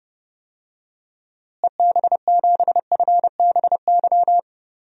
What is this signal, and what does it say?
Morse code, heard as a keyed tone, about 30 wpm, 700 Hz. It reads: EB7FBY
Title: DXCC-100 Call Signs - 6 Characters - Repeated 3x - 30wpm